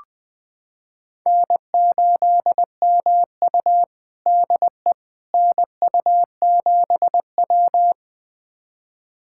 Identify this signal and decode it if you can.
Morse code, heard as a keyed tone, about 20 wpm, 700 Hz. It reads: N8MU DE NU7W
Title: USA Callsign DE Callsign - 4 Characters - 20wpm